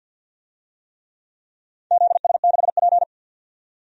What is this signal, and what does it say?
Morse code, heard as a keyed tone, about 53 wpm, 700 Hz. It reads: ZS6P
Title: DXCC-100 Call Signs - 4 Characters - 50wpm